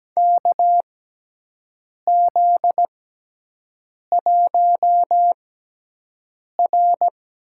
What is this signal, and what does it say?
Morse code, heard as a keyed tone, about 17 wpm, spaced as for 6 wpm, 700 Hz. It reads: KZ1R
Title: USA Call Signs - 4 Characters - 6x Character Spacing - 17wpm